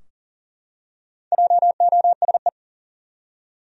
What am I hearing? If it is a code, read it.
Morse code, heard as a keyed tone, about 40 wpm, 700 Hz. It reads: JOSE
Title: USA Common Names - Max 4 Letters - 40wpm